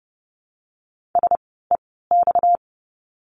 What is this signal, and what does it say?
Morse code, heard as a keyed tone, about 30 wpm, spaced as for 17 wpm, 700 Hz. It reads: SEX
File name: course / 30:10wpm / 230 Lesson 37f - Letter x - Words 30wpm